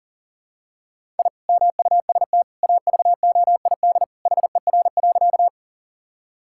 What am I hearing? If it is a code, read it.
Morse code, heard as a keyed tone, about 37 wpm, 690 Hz. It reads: I MUST AVOID HER.
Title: Sentences from Top 1000 Words - Review Part 6 - 40wpm